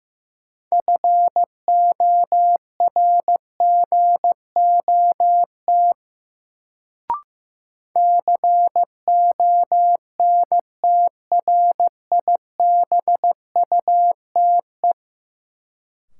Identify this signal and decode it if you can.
Morse code, heard as a keyed tone, about 15 wpm, 700 Hz. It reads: FORGOT CONTRIBUTE
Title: Sentences from Top 1400 Words - New Word Drill - 15wpm